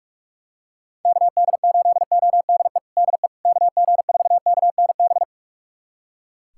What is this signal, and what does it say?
Morse code, heard as a keyed tone, about 41 wpm, 690 Hz. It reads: KD8ODE DE KG4KNB
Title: USA Callsign DE Callsign - 6 Characters - 45wpm